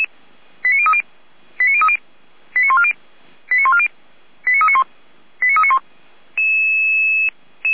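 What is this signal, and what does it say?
Five-tone selective call (ZVEI), heard as quick runs of five tones, tones of about 70 ms. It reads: 89030 89030 89140 89140 89391 89391
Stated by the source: Five-tone ZVEI selective call received by SDR.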